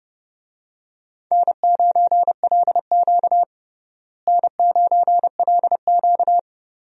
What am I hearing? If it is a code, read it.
Morse code, heard as a keyed tone, about 30 wpm, 700 Hz. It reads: N9LQ N9LQ
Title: USA Call Signs - 4 Characters - Repeated 3x - 30wpm